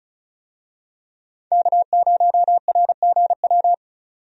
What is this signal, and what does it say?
Morse code, heard as a keyed tone, about 36 wpm, 700 Hz. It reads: K0RGW
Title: USA Call Signs - 5 Characters - 35wpm